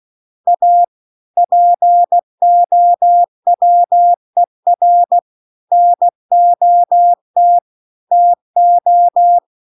Morse code, heard as a keyed tone, about 16 wpm, 700 Hz. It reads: A POWER NOT TO
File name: CW-The-Big-Twoheaded-River-2-16WPM